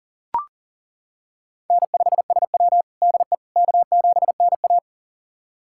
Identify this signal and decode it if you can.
Morse code, heard as a keyed tone, about 40 wpm, 700 Hz. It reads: N5SW DE K7NA
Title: USA Callsign DE Callsign - 4 Characters - 40wpm